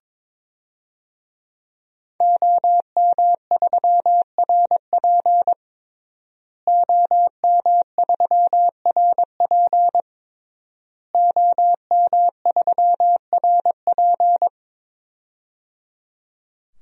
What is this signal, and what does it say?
Morse code, heard as a keyed tone, about 22 wpm, 700 Hz. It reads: OM3RP OM3RP OM3RP
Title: DXCC-100 Call Signs - 5 Characters - Repeated 3x - 22wpm